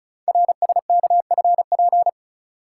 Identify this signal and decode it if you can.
Morse code, heard as a keyed tone, about 34 wpm, 700 Hz. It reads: RSKFP
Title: Code Groups - 5 Letters - 35wpm